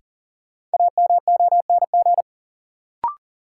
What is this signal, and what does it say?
Morse code, heard as a keyed tone, about 37 wpm, 700 Hz. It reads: AMONG
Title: Sentences from Top 500 Words - New Word Drill - 40wpm